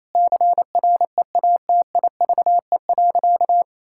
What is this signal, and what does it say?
Morse code, heard as a keyed tone, about 27 wpm, 700 Hz. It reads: CREATIVE.
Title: Sentences from Top 1000 Words - Review Part 5 - 28wpm